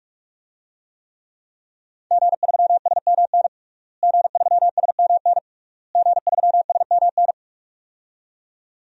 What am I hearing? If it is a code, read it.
Morse code, heard as a keyed tone, about 42 wpm, 700 Hz. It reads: G3SMN G3SMN G3SMN
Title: DXCC-100 Call Signs - 5 Characters - Repeated 3x - 45wpm